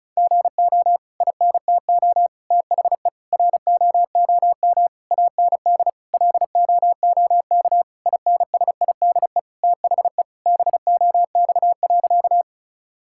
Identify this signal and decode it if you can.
Morse code, heard as a keyed tone, about 36 wpm, 700 Hz. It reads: GO INTO THE ROOM AND LOOK INSIDE THE BOX.
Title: Sentences from Top 1000 Words - Review Part 6 - 35wpm